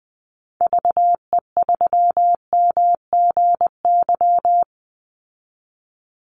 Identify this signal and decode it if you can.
Morse code, heard as a keyed tone, about 20 wpm, 700 Hz. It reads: VE3MGY
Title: Call Signs Repeated 3x - 20wpm